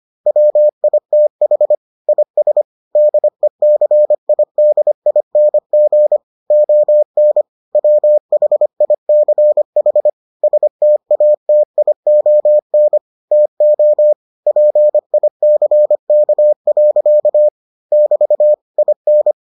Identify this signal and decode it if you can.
Morse code, heard as a keyed tone, about 25 wpm, 600 Hz. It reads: WITH IS DECIDING ON WHICH STATION TO PICK. <BT> IN